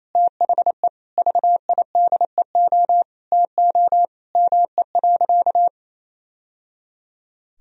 Morse code, heard as a keyed tone, about 28 wpm, 700 Hz. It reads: THE VIDEO TO ME.
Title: Sentences from Top 1000 Words - Review Part 4 - 28wpm